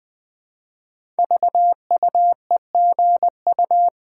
Morse code, heard as a keyed tone, about 20 wpm, 700 Hz. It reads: VUEGU
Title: Code Groups - 5 Letters - 20wpm